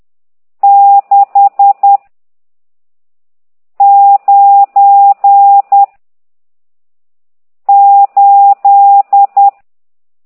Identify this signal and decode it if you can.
Morse code, heard as a keyed tone, about 10 wpm, 800 Hz. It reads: 6 9 8
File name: ct2idw / exerc14